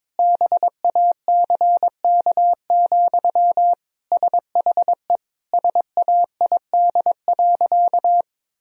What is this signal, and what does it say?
Morse code, heard as a keyed tone, about 22 wpm, 700 Hz. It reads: BACK, SHE SAID.
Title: Sentences from Top 1000 Words - Review Part 2 - 22wpm